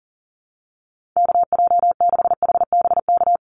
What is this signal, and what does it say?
Morse code, heard as a keyed tone, about 40 wpm, 700 Hz. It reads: KJ6HBK
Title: USA Call Signs 40wpm